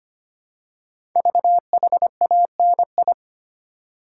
Morse code, heard as a keyed tone, about 25 wpm, 700 Hz. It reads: VHANI
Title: Code Groups - 5 Letters - 25wpm